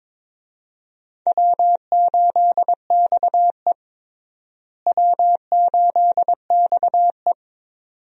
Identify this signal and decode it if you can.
Morse code, heard as a keyed tone, about 22 wpm, 700 Hz. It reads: W8XE W8XE
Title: USA Call Signs - 4 Characters - Repeated 2x - 22wpm